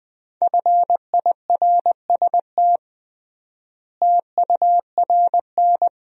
Morse code, heard as a keyed tone, about 20 wpm, 700 Hz. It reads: FIRST TURN
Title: 3x Word-Spacing - Sentences from Top 300 Words - 20wpm